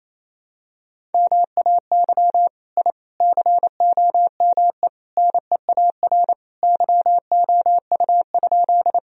Morse code, heard as a keyed tone, about 28 wpm, 700 Hz. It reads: MAY I COME NEAR YOU?
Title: Sentences from Top 1000 Words - Review Part 1 - 28wpm